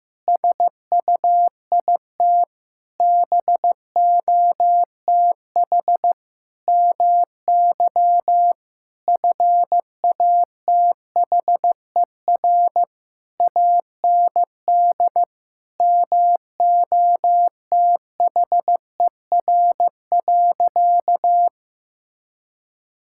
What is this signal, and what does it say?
Morse code, heard as a keyed tone, about 15 wpm, 700 Hz. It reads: SUIT BOTH MY FATHER AND MOTHER.
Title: Sentences from Top 1600 Words 15wpm